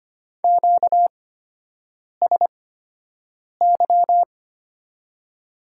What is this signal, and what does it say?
Morse code, heard as a keyed tone, about 25 wpm, spaced as for 7 wpm, 700 Hz. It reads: QSY
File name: Top 90 QSO Elements - 8x Character Spacing - 25wpm